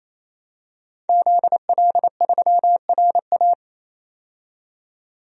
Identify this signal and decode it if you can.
Morse code, heard as a keyed tone, about 28 wpm, 700 Hz. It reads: ZL3RA
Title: DXCC-100 Call Signs - 5 Characters - 28wpm